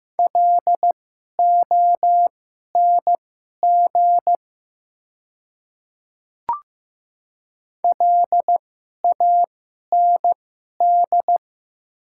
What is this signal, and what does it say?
Morse code, heard as a keyed tone, about 15 wpm, spaced as for 11 wpm, 700 Hz. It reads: LONG LAND
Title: 2x Character Spacing - Top 200 Words - 15wpm